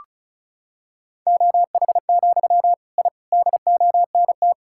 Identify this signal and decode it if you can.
Morse code, heard as a keyed tone, about 36 wpm, 700 Hz. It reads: OH, I DONT
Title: Sentences from Top 1000 Words - Review Part 6 - 35wpm